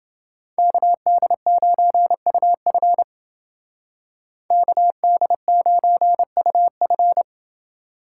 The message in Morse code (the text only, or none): KD9UF KD9UF